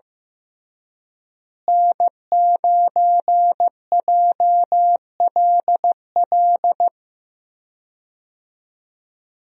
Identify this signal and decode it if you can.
Morse code, heard as a keyed tone, about 15 wpm, 700 Hz. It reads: N9JLL
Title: USA Call Signs - 5 Characters - Repeated 3x - 15wpm